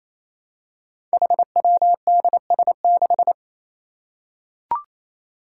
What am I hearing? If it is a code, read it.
Morse code, heard as a keyed tone, about 28 wpm, 700 Hz. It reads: HWDS6
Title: Five Challenging Letter-Number Combinations 28wpm